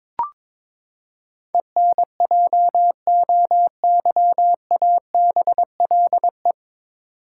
Morse code, heard as a keyed tone, about 22 wpm, 700 Hz. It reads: ENJOYABLE
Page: Morse Code Ninja practice 22wpm